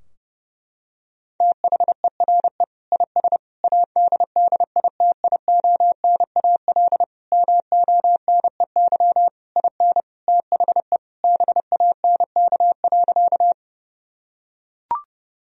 Morse code, heard as a keyed tone, about 30 wpm, 700 Hz. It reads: THERE IS ADDITIONAL MONEY IN THE BANK.
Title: Sentences from Top 1000 Words - Review Part 4 - 30wpm